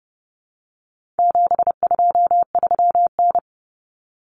Morse code, heard as a keyed tone, about 30 wpm, 700 Hz. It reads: 723N
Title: Common Cut Numbers - 4-Digit 30wpm